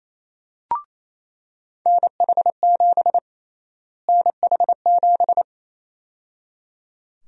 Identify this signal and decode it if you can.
Morse code, heard as a keyed tone, about 27 wpm, 700 Hz. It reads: NH7 NH7
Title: USA Call Sign Prefixes - Repeated 2x - 28wpm